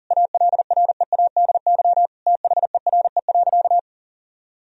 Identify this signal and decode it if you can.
Morse code, heard as a keyed tone, about 40 wpm, 700 Hz. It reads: ALREADY THERE.
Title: Sentences from Top 1000 Words - Review Part 1 - 40wpm